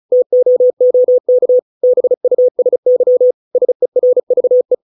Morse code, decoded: TOOK BUSY SERVE

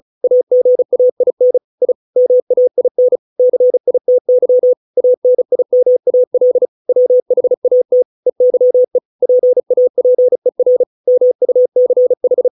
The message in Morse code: AGAIN I MAIN CITY ANIMAL WHAT EYE PAPER MUCH

